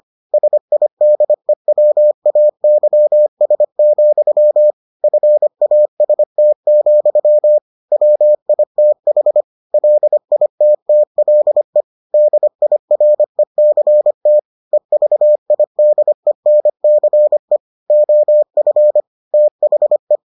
Morse code, heard as a keyed tone, about 25 wpm, 600 Hz. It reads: SIDEWAYS, FAST, WITH LITTLE DIRECT EVIDENCE OF THE